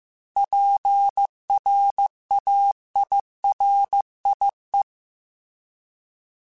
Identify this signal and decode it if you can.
Morse code, heard as a keyed tone, about 15 wpm, 800 Hz. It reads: PRAIRIE
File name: fromSMs / FromSMs mp3s / Cities@13